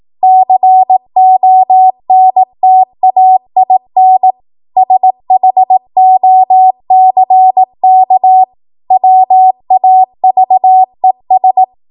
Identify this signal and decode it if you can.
Morse code, heard as a keyed tone, about 18 wpm, 750 Hz. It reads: CONTAIN SHOCK WAVES
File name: Morse / 18WPM